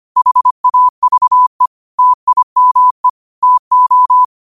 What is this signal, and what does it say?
Morse code, heard as a keyed tone, about 25 wpm, 1000 Hz. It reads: HAVE TIME TO
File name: qotdfiles / 25wpm